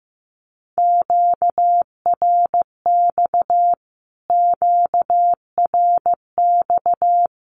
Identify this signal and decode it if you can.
Morse code, heard as a keyed tone, about 15 wpm, 700 Hz. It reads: QRX QRX
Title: Q-Codes Repeated 3x 15wpm